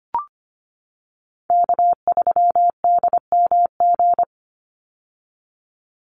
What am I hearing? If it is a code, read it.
Morse code, heard as a keyed tone, about 25 wpm, 700 Hz. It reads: K3DMG
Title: Call Signs - 25wpm